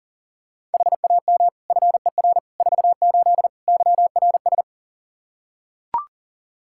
Morse code, heard as a keyed tone, about 40 wpm, 700 Hz. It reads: HAM FER 48 YRS
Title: Practice QSO - Exchange 3 - More Info - 40wpm